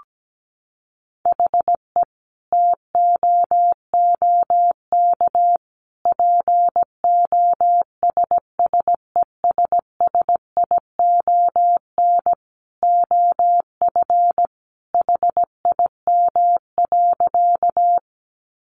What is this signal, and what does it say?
Morse code, heard as a keyed tone, about 17 wpm, 700 Hz. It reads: HE TOOK POSSESSION OF HIM.
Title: Sentences from Top 1400 Words 17wpm